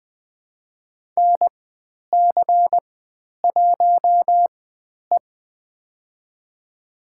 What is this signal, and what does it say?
Morse code, heard as a keyed tone, about 20 wpm, spaced as for 10 wpm, 700 Hz. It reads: NC1E
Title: 095 Lesson 16g - Letter c - Call Signs 20wpm